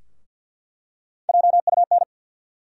Morse code, heard as a keyed tone, about 53 wpm, 700 Hz. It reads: JUN
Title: USA Call Sign Postfixes - 3 Letters 50wpm